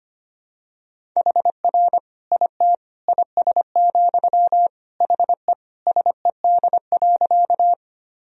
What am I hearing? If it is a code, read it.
Morse code, heard as a keyed tone, about 25 wpm, 700 Hz. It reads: HR IT IS, HE SED.
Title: Sentences from Top 200 Words and Abbreviations 25wpm